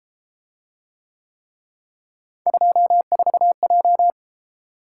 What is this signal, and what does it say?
Morse code, heard as a keyed tone, about 32 wpm, 700 Hz. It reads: 24J